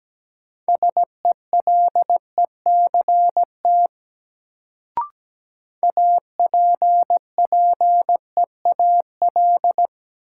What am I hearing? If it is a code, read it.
Morse code, heard as a keyed tone, about 17 wpm, 700 Hz. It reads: SELECT APPEAL
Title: Sentences from Top 1300 Words - New Word Drill - 17wpm